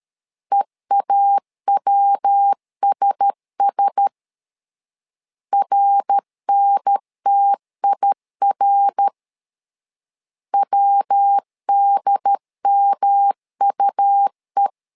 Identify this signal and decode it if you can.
Morse code, heard as a keyed tone, about 13 wpm, 800 Hz. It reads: EAWSS RNTIR WDMUE
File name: LEZIONI / 4 Q1S 12wpm 1802